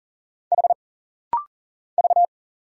Morse code, heard as a keyed tone, about 40 wpm, 700 Hz. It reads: H V
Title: Single Letter-Number - Rapid-Fire 40wpm